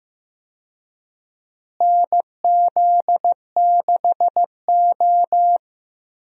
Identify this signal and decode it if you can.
Morse code, heard as a keyed tone, about 15 wpm, 700 Hz. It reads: NZ6O
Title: USA Call Signs - 4 Characters - Repeated 3x - 15wpm